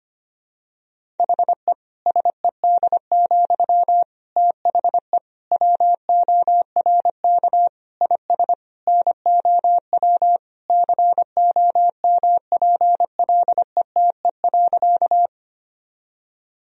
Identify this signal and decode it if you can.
Morse code, heard as a keyed tone, about 25 wpm, 700 Hz. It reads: HE SED, THE WORK IS NOW COMPLETE.
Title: Sentences from Top 400 Words and Abbreviations 25wpm